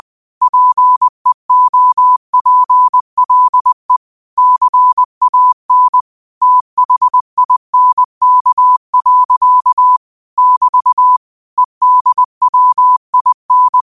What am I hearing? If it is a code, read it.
Morse code, heard as a keyed tone, about 20 wpm, 1000 Hz. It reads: PEOPLE CAN THINK. <BT> EDWIN